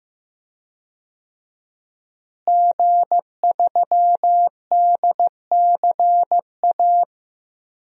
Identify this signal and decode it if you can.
Morse code, heard as a keyed tone, about 15 wpm, 700 Hz. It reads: G3DCA
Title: DXCC-100 Call Signs - 5 Characters - Repeated 3x - 15wpm